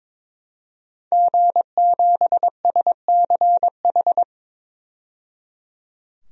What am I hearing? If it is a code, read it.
Morse code, heard as a keyed tone, about 21 wpm, 700 Hz. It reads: G7SCH
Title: DXCC-100 Call Signs - 5 Characters - 22wpm